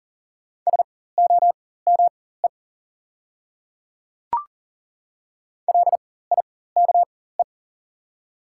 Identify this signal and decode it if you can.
Morse code, heard as a keyed tone, about 39 wpm, spaced as for 19 wpm, 700 Hz. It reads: SOME LIKE
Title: 4x Character Spacing - Top 300 Words - 40wpm